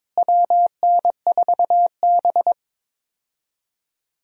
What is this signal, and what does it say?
Morse code, heard as a keyed tone, about 22 wpm, 700 Hz. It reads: WN4B
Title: USA Call Signs - 4 Characters - Repeated 3x - 22wpm